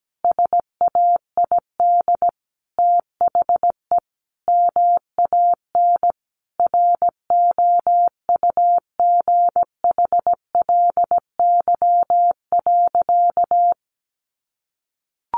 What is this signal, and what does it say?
Morse code, heard as a keyed tone, about 17 wpm, 700 Hz. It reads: SAID THE MAN ROUGHLY.
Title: Sentences from Top 1500 Words 17wpm